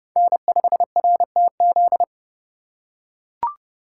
Morse code, heard as a keyed tone, about 30 wpm, 700 Hz. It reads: N5RTZ
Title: USA Call Signs - 2x Character Spacing - 30wpm